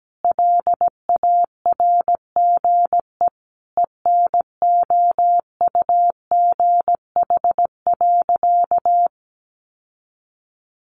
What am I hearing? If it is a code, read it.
Morse code, heard as a keyed tone, about 17 wpm, 700 Hz. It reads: LARGE ENOUGH.
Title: Sentences from Top 1500 Words 17wpm